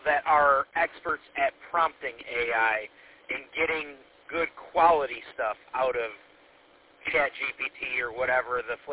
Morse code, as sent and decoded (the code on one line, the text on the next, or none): none